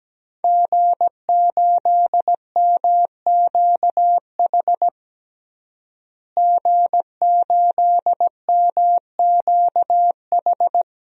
Morse code, heard as a keyed tone, about 17 wpm, 700 Hz. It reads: G8MQH G8MQH
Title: DXCC-100 Call Signs - 5 Characters - Repeated 3x - 17wpm